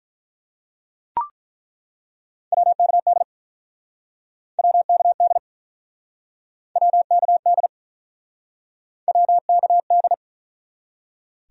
Morse code, heard as a keyed tone, about 48 wpm, 700 Hz. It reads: WKD WKD WKD WKD